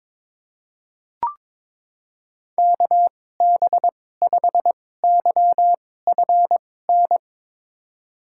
Morse code, heard as a keyed tone, about 22 wpm, spaced as for 16 wpm, 700 Hz. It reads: KB5YFN
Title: USA Call Signs - 6 Characters - 2x Character Spacing - 22wpm